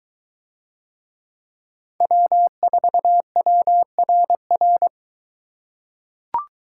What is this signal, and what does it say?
Morse code, heard as a keyed tone, about 22 wpm, 700 Hz. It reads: W4WRR